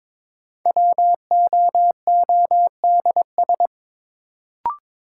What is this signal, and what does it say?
Morse code, heard as a keyed tone, about 22 wpm, 700 Hz. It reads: WOODS
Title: Sentences from Top 1000 Words - Review Part 2 - New Word Drill - 22wpm